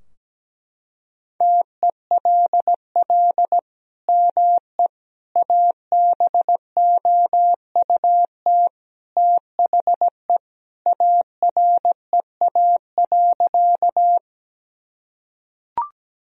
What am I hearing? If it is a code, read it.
Morse code, heard as a keyed tone, about 17 wpm, 700 Hz. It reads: TELL ME ABOUT THE AREA.
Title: Sentences from Top 1000 Words - Review Part 2 - 17wpm